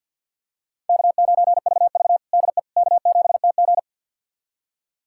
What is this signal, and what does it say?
Morse code, heard as a keyed tone, about 49 wpm, 700 Hz. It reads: K9VV DE K7TG
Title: USA Callsign DE Callsign - 4 Characters - 50wpm